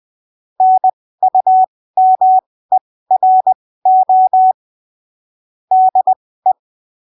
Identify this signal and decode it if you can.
Morse code, heard as a keyed tone, about 20 wpm, spaced as for 15 wpm, 750 Hz. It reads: NUMERO DE
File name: arquivos / texto / texto 22 20ppm